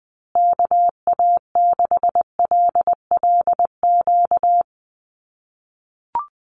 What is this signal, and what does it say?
Morse code, heard as a keyed tone, about 19 wpm, 700 Hz. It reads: KA6LLQ
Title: USA Call Signs Repeated 3x 20wpm